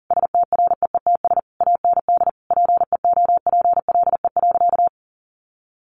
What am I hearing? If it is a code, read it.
Morse code, heard as a keyed tone, about 40 wpm, 700 Hz. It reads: STREETS AND PEOPLE.
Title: A Sound Of Thunder 40wpm